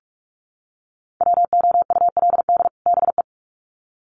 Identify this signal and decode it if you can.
Morse code, heard as a keyed tone, about 44 wpm, 700 Hz. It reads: WOULD BE
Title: Sets of 2 Words 45wpm